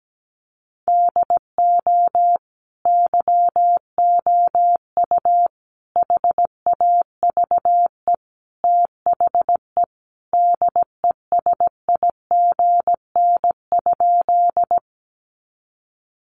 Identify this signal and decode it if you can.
Morse code, heard as a keyed tone, about 17 wpm, 700 Hz. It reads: DO YOU HAVE THE DESIGN?
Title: Sentences from Top 800 Words 17wpm